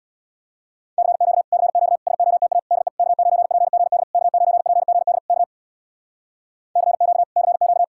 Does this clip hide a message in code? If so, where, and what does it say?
Yes: Morse code from 1.0 to 7.8 s, heard as a keyed tone, about 65 wpm, 700 Hz. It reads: CQ CQ SOTA DE K9PGD K9PGD K CQ CQ